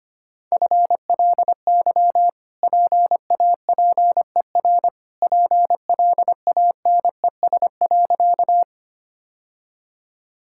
Morse code, heard as a keyed tone, about 25 wpm, 700 Hz. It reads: FLY PAPER PLANES.
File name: Easy Sentences with Hints - 25wpm